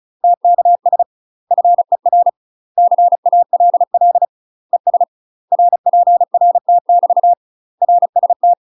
Morse code, heard as a keyed tone, about 35 wpm, 700 Hz. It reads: TKS FER CALL ES RPRT <BT> RST
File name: UK MASTER QSOs TRAINING / 35WPM / UK MASTER QSOs 0032